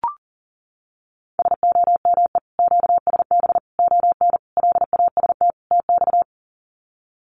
Morse code, heard as a keyed tone, about 39 wpm, 700 Hz. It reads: SOME QSB ON LAST TX
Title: Common QSO Phrases 40wpm